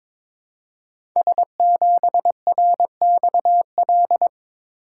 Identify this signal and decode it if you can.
Morse code, heard as a keyed tone, about 22 wpm, 700 Hz. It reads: S7RXL
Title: Code Groups - 5 Letter:Numbers - 22wpm